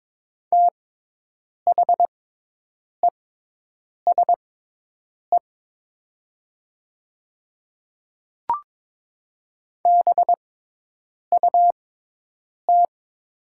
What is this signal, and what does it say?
Morse code, heard as a keyed tone, about 22 wpm, spaced as for 8 wpm, 700 Hz. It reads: THESE BUT
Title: 6x Character Spacing - Top 300 Words - 22wpm